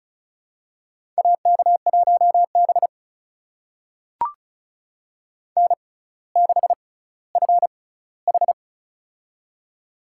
Morse code, heard as a keyed tone, about 36 wpm, spaced as for 18 wpm, 700 Hz. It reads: AK1B N6FH
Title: USA Call Signs - 4 Characters - 6x Character Spacing - 35wpm